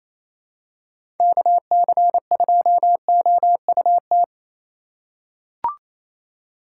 Morse code, heard as a keyed tone, about 28 wpm, 700 Hz. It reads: KC2OUT
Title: USA Call Signs - 6 Characters - 2x Character Spacing - 28wpm